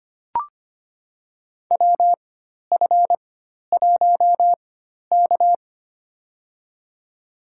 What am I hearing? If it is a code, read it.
Morse code, heard as a keyed tone, about 25 wpm, spaced as for 12 wpm, 700 Hz. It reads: WF1K
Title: USA Call Signs - 4 Characters - 4x Character Spacing - 25wpm